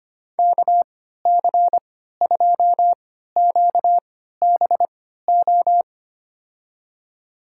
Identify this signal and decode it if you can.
Morse code, heard as a keyed tone, about 25 wpm, spaced as for 14 wpm, 700 Hz. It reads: KC2QBO
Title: USA Call Signs - 6 Characters - 3x Character Spacing - 25wpm